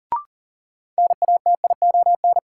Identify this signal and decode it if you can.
Morse code, heard as a keyed tone, about 40 wpm, 700 Hz. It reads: NATION
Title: Sentences from Top 900 Words - New Word Drill - 40wpm